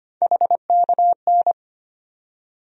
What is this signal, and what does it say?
Morse code, heard as a keyed tone, about 25 wpm, 700 Hz. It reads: HKN
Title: USA Call Sign Postfixes - 3 Letters 25wpm